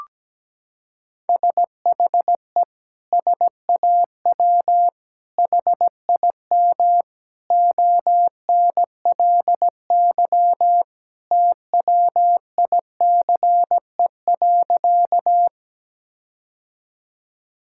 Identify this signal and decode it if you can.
Morse code, heard as a keyed tone, about 17 wpm, 700 Hz. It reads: SHE SAW HIM ONLY TWICE.
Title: Sentences from Top 1600 Words 17wpm